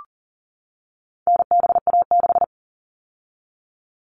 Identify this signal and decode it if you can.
Morse code, heard as a keyed tone, about 40 wpm, 700 Hz. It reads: NBA6
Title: All Cut Numbers - 4-Digit 40wpm